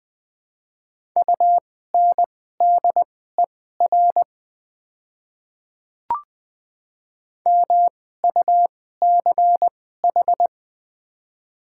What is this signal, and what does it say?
Morse code, heard as a keyed tone, about 20 wpm, spaced as for 14 wpm, 700 Hz. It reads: UNDER MUCH
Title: 2x Character Spacing - Top 300 Words - 20wpm